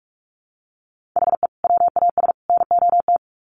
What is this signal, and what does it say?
Morse code, heard as a keyed tone, about 44 wpm, 700 Hz. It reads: HE WAS NOT
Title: Sets of 3 Words 45wpm